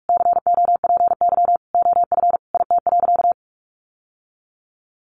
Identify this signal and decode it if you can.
Morse code, heard as a keyed tone, about 44 wpm, 700 Hz. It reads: COPY OF IT.
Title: Sentences from Top 1400 Words 45wpm